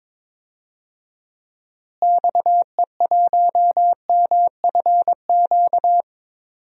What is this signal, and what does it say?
Morse code, heard as a keyed tone, about 22 wpm, 700 Hz. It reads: XE1MFQ
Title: DXCC-100 Call Signs - 6 Characters - Repeated 2x - 22wpm